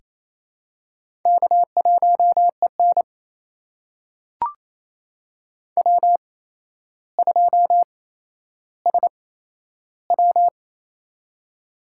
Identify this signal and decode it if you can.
Morse code, heard as a keyed tone, about 28 wpm, spaced as for 12 wpm, 700 Hz. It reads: K1EN W2SW
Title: USA Call Signs - 4 Characters - 8x Character Spacing - 28wpm